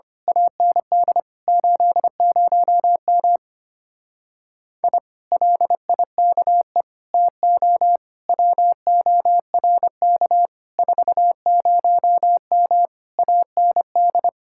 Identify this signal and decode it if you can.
Morse code, heard as a keyed tone, about 24 wpm, 700 Hz. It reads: AND 80M I LIKE TO WORK 40M AND
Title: Speed-Racing Common QSO Phrases - 25wpm